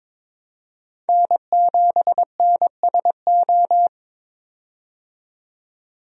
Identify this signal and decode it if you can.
Morse code, heard as a keyed tone, about 21 wpm, 700 Hz. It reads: N7NSO
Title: USA Call Signs - 5 Characters - 22wpm